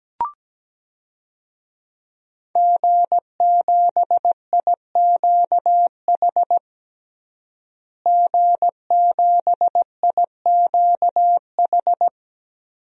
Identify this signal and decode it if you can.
Morse code, heard as a keyed tone, about 17 wpm, 700 Hz. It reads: G7IQH G7IQH
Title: DXCC-100 Call Signs - 5 Characters - Repeated 2x - 17wpm